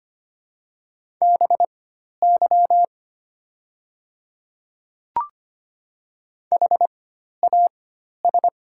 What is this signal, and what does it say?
Morse code, heard as a keyed tone, about 25 wpm, spaced as for 12 wpm, 700 Hz. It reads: BY HAS
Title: 4x Character Spacing - Top 100 Words - 25wpm